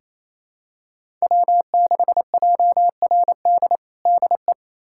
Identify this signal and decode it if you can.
Morse code, heard as a keyed tone, about 28 wpm, 700 Hz. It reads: W6JRD DE
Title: USA Callsign DE Callsign 28wpm